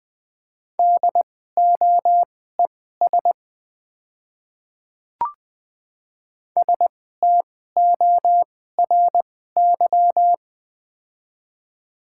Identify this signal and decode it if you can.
Morse code, heard as a keyed tone, about 20 wpm, spaced as for 14 wpm, 700 Hz. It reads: DOES STORY
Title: 2x Character Spacing - Top 300 Words - 20wpm